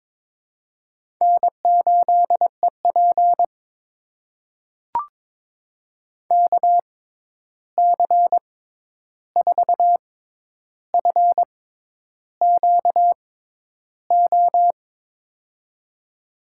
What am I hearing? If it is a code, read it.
Morse code, heard as a keyed tone, about 22 wpm, spaced as for 10 wpm, 700 Hz. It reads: N8EP KC4FQO
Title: USA Call Signs - 6x Character Spacing - 22wpm